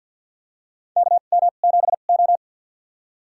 Morse code, heard as a keyed tone, about 50 wpm, spaced as for 36 wpm, 700 Hz. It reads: KM7O
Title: USA Call Signs - 4 Characters - 2x Character Spacing - 50wpm